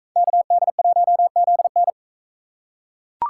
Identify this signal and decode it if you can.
Morse code, heard as a keyed tone, about 42 wpm, 700 Hz. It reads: KD1ZN